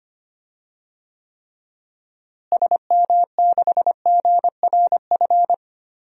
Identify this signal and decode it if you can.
Morse code, heard as a keyed tone, about 25 wpm, 700 Hz. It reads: SM6GRF
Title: DXCC-100 Call Signs - 6 Characters - Repeated 2x - 25wpm